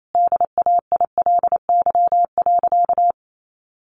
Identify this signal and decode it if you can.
Morse code, heard as a keyed tone, about 28 wpm, 700 Hz. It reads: DAILY.
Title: Sentences from Top 1100 Words 28wpm